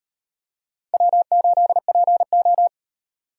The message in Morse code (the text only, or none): W8PO